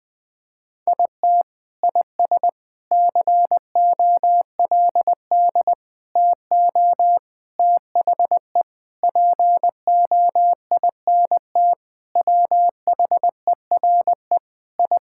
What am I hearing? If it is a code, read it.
Morse code, heard as a keyed tone, about 20 wpm, 700 Hz. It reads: IT IS COLD TO THE POINT WHERE I
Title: Sentences with 4-Word Phrases - Spoken Beginning - 20wpm